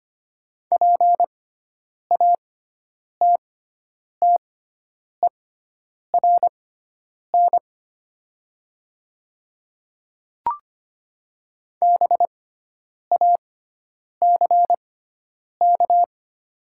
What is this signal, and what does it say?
Morse code, heard as a keyed tone, about 25 wpm, spaced as for 9 wpm, 700 Hz. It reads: PATTERN BACK
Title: 6x Character Spacing - Top 500 Words - 25wpm